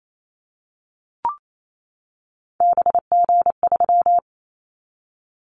Morse code, heard as a keyed tone, about 28 wpm, 700 Hz. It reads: BG3
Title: Callsign Prefixes for Top 100 DXCC - 28wpm